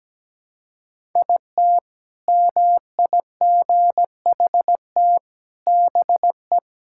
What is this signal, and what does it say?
Morse code, heard as a keyed tone, about 17 wpm, 700 Hz. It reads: IT MIGHT BE
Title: Sentences from Top 1000 Words - Review Part 6 - 17wpm